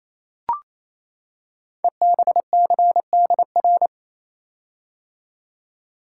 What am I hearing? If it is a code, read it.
Morse code, heard as a keyed tone, about 27 wpm, 700 Hz. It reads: EBCDR
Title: Code Groups - 5 Letters - 28wpm